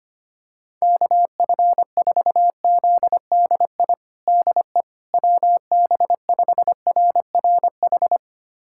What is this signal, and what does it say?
Morse code, heard as a keyed tone, about 25 wpm, 700 Hz. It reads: KF4ZDI DE WB5RRH
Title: USA Callsign DE Callsign - 6 Characters - 25wpm